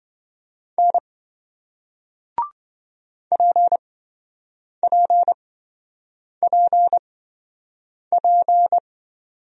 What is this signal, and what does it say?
Morse code, heard as a keyed tone, about 28 wpm, spaced as for 7 wpm, 700 Hz. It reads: N PPPP